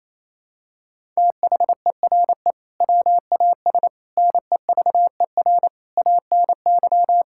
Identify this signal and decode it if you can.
Morse code, heard as a keyed tone, about 28 wpm, 700 Hz. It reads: THERE WAS NEVER ANY
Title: Sentences from Top 1000 Words - Review Part 6 - 28wpm